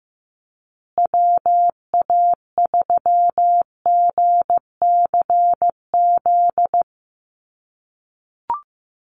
Morse code, heard as a keyed tone, about 15 wpm, 700 Hz. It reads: WA3GCZ
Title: USA Call Signs Repeated 3x 15wpm